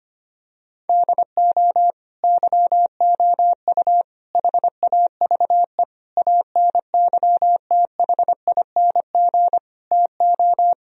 Morse code, heard as a keyed tone, about 25 wpm, 700 Hz. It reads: DO YOU HAVE ANYTHING TO
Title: Sentences from Top 1000 Words - Review Part 1 - 25wpm